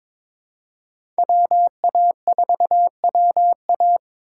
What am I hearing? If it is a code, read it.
Morse code, heard as a keyed tone, about 22 wpm, 700 Hz. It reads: WA4WA
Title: USA Call Signs - 4x Character Spacing - 22wpm